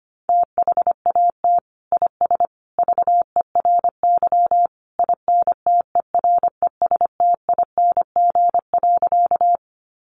Morse code, heard as a keyed tone, about 25 wpm, 700 Hz. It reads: THAT IS VERY INTERESTING.
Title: Sentences from Top 1000 Words 25wpm